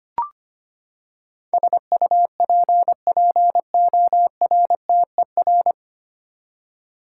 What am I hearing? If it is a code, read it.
Morse code, heard as a keyed tone, about 25 wpm, 700 Hz. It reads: SUPPORTER
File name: Nine Letter Words - 25wpm